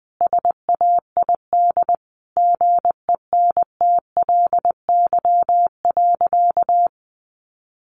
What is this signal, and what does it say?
Morse code, heard as a keyed tone, about 20 wpm, 700 Hz. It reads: SAID GENTLY.
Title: Sentences from Top 1500 Words 20wpm